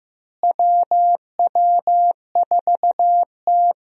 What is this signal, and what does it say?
Morse code, heard as a keyed tone, about 15 wpm, 700 Hz. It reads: WW4T
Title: USA Call Signs - 4 Characters - Repeated 3x - 15wpm